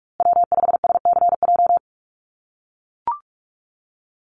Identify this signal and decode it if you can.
Morse code, heard as a keyed tone, about 44 wpm, 700 Hz. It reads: W5SCJ
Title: USA Call Signs Repeated 2x 45wpm